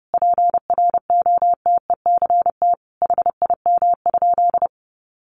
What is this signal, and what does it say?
Morse code, heard as a keyed tone, about 30 wpm, 700 Hz. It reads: PROTECT HIM?
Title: Sentences from Top 900 Words 30wpm